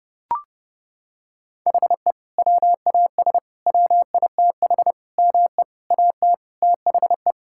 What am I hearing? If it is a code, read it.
Morse code, heard as a keyed tone, about 30 wpm, 700 Hz. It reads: HE WAS WITH ME AT THE
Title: Sentences from Top 1000 Words - Review Part 5 - 30wpm